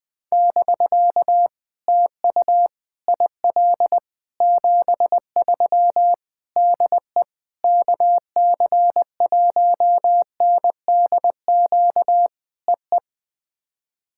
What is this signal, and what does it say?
Morse code, heard as a keyed tone, about 20 wpm, 700 Hz. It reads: <BK> TU IL 73 DE KC1NDQ EE
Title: POTA QSO Practice - 20wpm